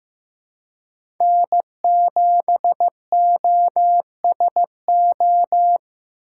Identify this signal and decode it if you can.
Morse code, heard as a keyed tone, about 15 wpm, 700 Hz. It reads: N7OSO
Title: USA Call Signs - 5 Characters - 2x Character Spacing - 15wpm